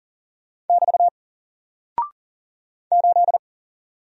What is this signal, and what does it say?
Morse code, heard as a keyed tone, about 40 wpm, 700 Hz. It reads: <BT> 8
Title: Single Character (Letter:Number:Punctuation:Prosign) 40wpm